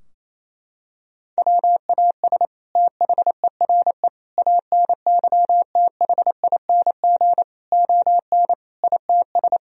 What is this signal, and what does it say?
Morse code, heard as a keyed tone, about 28 wpm, 700 Hz. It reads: WAS THERE ANYTHING ON ITS
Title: Sentences from Top 1000 Words - Review Part 3 - 28wpm